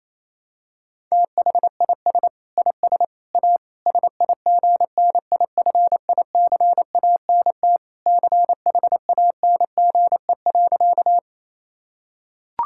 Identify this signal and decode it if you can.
Morse code, heard as a keyed tone, about 27 wpm, 700 Hz. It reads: THIS IS A SIGNIFICANT CHANGE.
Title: Sentences from Top 1000 Words - Review Part 6 - 28wpm